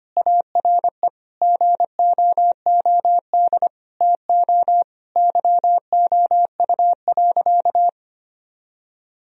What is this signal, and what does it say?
Morse code, heard as a keyed tone, about 25 wpm, 700 Hz. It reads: ARE GOOD TO YOU.
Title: Sentences from Top 1000 Words - Review Part 1 - 25wpm